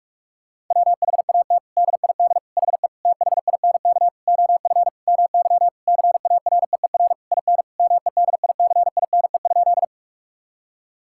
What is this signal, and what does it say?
Morse code, heard as a keyed tone, about 43 wpm, 700 Hz. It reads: WHAT DID HE THINK OF MY CAREER IN MEDICINE?
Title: Sentences from Top 1000 Words - Review Part 6 - 45wpm